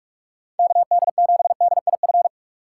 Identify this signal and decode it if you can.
Morse code, heard as a keyed tone, about 59 wpm, spaced as for 52 wpm, 700 Hz. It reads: KD7DIF
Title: Speed Racing - USA 2x3 Call Signs 45wpm